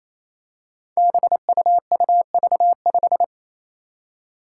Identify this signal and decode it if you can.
Morse code, heard as a keyed tone, about 28 wpm, 700 Hz. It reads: BUUV5
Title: Five Challenging Letter-Number Combinations 28wpm